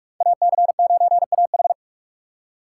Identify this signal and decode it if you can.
Morse code, heard as a keyed tone, about 44 wpm, 700 Hz. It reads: AC9AH
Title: USA Call Signs - 6x Character Spacing - 45wpm